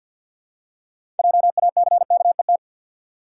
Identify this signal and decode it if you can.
Morse code, heard as a keyed tone, about 52 wpm, 700 Hz. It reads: JACKET
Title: Six Letter Words - 50wpm